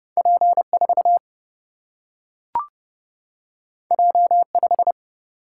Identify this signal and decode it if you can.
Morse code, heard as a keyed tone, about 29 wpm, 700 Hz. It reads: P4 J5
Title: Single-Letter Single-Number 30wpm